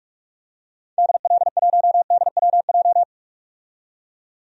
Morse code, heard as a keyed tone, about 44 wpm, 690 Hz. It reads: DL1DWJ
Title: DXCC-100 Call Signs - 6 Characters - 45wpm